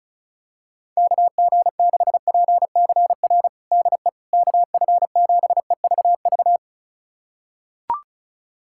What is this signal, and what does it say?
Morse code, heard as a keyed tone, about 34 wpm, 700 Hz. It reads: KG6PCR DE KF7EVV
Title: USA Callsign DE Callsign - 6 Characters - 35wpm